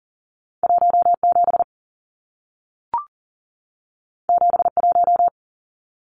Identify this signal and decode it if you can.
Morse code, heard as a keyed tone, about 40 wpm, 700 Hz. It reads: 17 71
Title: Numbers - 2-Digit 40wpm